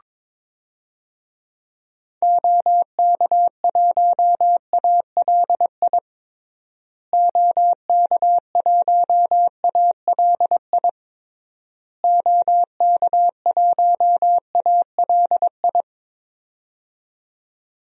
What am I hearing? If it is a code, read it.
Morse code, heard as a keyed tone, about 22 wpm, 700 Hz. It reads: OK1ALI OK1ALI OK1ALI
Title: DXCC-100 Call Signs - 6 Characters - Repeated 3x - 22wpm